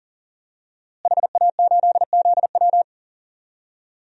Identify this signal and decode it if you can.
Morse code, heard as a keyed tone, about 40 wpm, 700 Hz. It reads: HA8ZW
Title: DXCC-100 Call Signs - 5 Characters - Repeated 2x - 40wpm